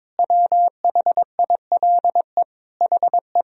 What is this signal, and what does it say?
Morse code, heard as a keyed tone, about 22 wpm, 700 Hz. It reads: WHILE HE
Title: Sentences from Top 1000 Words - Review Part 1 - 22wpm